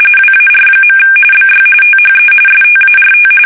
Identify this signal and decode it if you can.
RTTY, about 45.45 baud. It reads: WAR. BY THE START
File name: fsk850 / battleships